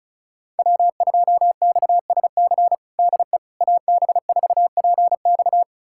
Morse code, heard as a keyed tone, about 36 wpm, 700 Hz. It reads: W2XSC DE AB4PX
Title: USA Callsign DE Callsign - 5 Characters - 35wpm